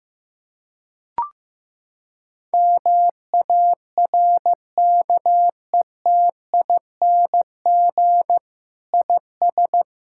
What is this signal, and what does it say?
Morse code, heard as a keyed tone, about 15 wpm, 700 Hz. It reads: MARKETING IS
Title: Sentences from Top 1000 Words - Review Part 5 - 15wpm